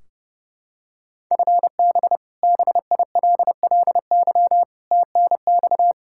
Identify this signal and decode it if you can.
Morse code, heard as a keyed tone, about 32 wpm, 700 Hz. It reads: FB BILLY TNX
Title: K1USN SST QSO Practice 30wpm